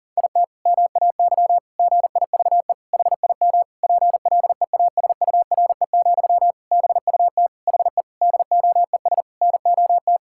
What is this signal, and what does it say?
Morse code, heard as a keyed tone, about 40 wpm, 700 Hz. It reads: IT MAY GIVE HIM PLEASURE, BUT HE DOES NOT
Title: Sentences from Top 1600 Words 40wpm